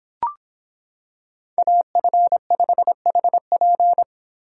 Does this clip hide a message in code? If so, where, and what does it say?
Yes: Morse code from 1.6 to 4.0 s, heard as a keyed tone, about 26 wpm, 700 Hz. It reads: AF5HP